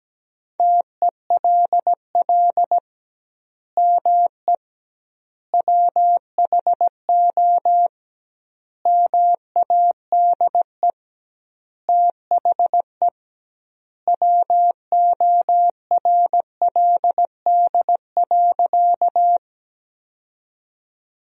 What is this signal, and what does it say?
Morse code, heard as a keyed tone, about 17 wpm, 700 Hz. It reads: TELL ME WHO MADE THE WORLD.
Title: 2x Word-Spacing - Sentences from Top 200 Words - 17wpm